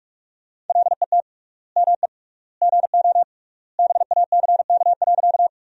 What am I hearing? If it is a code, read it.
Morse code, heard as a keyed tone, about 47 wpm, 700 Hz. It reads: LET ME GO BACK.
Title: 3x Word-Spacing - Sentences from Top 300 Words - 45wpm